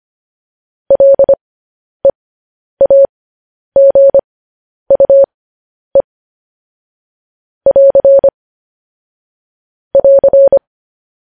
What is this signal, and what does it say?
Morse code, heard as a keyed tone, about 25 wpm, spaced as for 10 wpm, 550 Hz. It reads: LEAGUE <AR> <AR>
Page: news headlines in morse code at 10 wpm